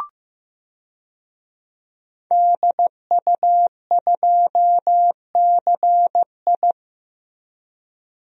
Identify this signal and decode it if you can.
Morse code, heard as a keyed tone, about 15 wpm, 700 Hz. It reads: DU2CI
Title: DXCC-100 Call Signs - 5 Characters - Repeated 3x - 15wpm